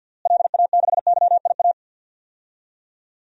Morse code, heard as a keyed tone, about 52 wpm, 700 Hz. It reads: LA6YIA
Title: DXCC-100 Call Signs - 6 Characters - Repeated 3x - 50wpm